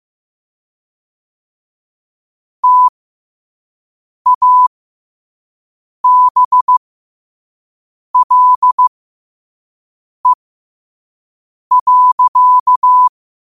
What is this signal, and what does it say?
Morse code, heard as a keyed tone, about 15 wpm, spaced as for 5 wpm, 1000 Hz. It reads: TABLE.
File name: qotdfiles / 5wpm